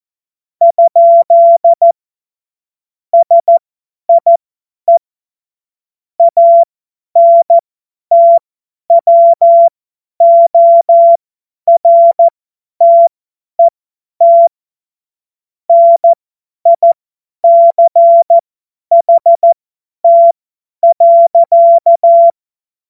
Morse code, heard as a keyed tone, about 14 wpm, spaced as for 10 wpm, 680 Hz. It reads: ? SIE ANTWORTET NICHT.
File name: Morsekurs / Witze / Witz 06. September 2022 50 F70